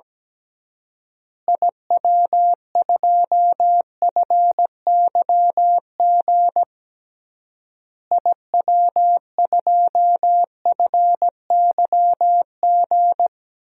Morse code, heard as a keyed tone, about 17 wpm, 700 Hz. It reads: IW2FYG IW2FYG
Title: DXCC-100 Call Signs - 6 Characters - Repeated 3x - 17wpm